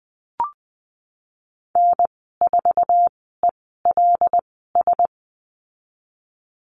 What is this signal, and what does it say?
Morse code, heard as a keyed tone, about 20 wpm, spaced as for 15 wpm, 700 Hz. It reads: N4ELS
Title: 231 Lesson 37g - Letter x - Call Signs 20wpm